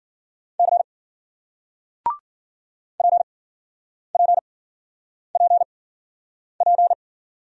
Morse code, heard as a keyed tone, about 52 wpm, spaced as for 9 wpm, 700 Hz. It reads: C PPPP